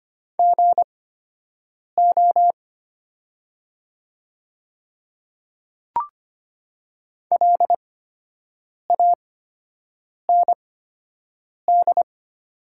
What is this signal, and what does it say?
Morse code, heard as a keyed tone, about 25 wpm, spaced as for 7 wpm, 700 Hz. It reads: GO LAND
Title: 8x Character Spacing - Top 500 Words - 25wpm